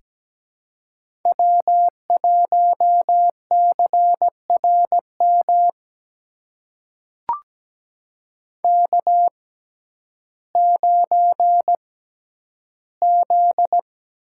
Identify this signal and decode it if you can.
Morse code, heard as a keyed tone, about 17 wpm, spaced as for 10 wpm, 700 Hz. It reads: W1CRM K9Z
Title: USA Call Signs - 6x Character Spacing - 17wpm